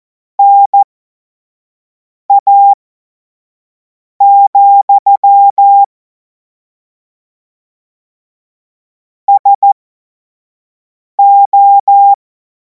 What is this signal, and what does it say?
Morse code, heard as a keyed tone, about 14 wpm, spaced as for 5 wpm, 800 Hz. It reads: NA, SO
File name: Morsekurs / Witze / Witz 03. März 2026 25 F70